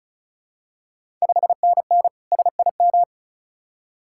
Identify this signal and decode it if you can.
Morse code, heard as a keyed tone, about 36 wpm, 700 Hz. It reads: 5NN SIM
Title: Ontario QSO Party Exchange 35wpm